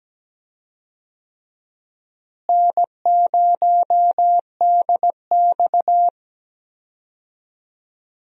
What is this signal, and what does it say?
Morse code, heard as a keyed tone, about 17 wpm, 700 Hz. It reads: N0DX
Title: USA Call Signs - 4 Characters - Repeated 2x - 17wpm